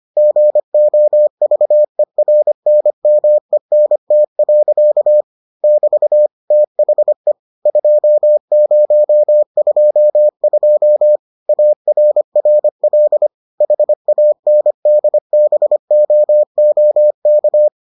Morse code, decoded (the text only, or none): GOVERNMENT. <BT> THE 2022 ARRL HANDBOOK